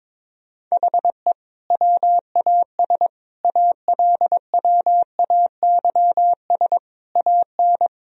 Morse code, heard as a keyed tone, about 21 wpm, 700 Hz. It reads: HE WAS ALWAYS AN
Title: Sentences from Top 1000 Words - Review Part 2 - 22wpm